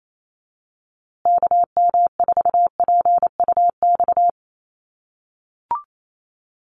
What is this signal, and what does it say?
Morse code, heard as a keyed tone, about 28 wpm, 700 Hz. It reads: KM4PUX
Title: USA Call Signs Repeated 3x 28wpm